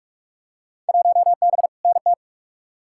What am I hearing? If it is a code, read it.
Morse code, heard as a keyed tone, about 44 wpm, 700 Hz. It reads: 1B NT
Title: Field Day Exchanges Encore - 45wpm